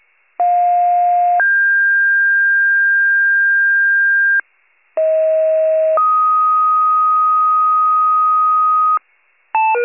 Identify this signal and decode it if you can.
Two-tone page, heard as a one-second tone, then a three-second tone, about 710 then 1600 Hz, about 640 then 1250 Hz, with noise. It is heard from 0.4 to 9.0 s.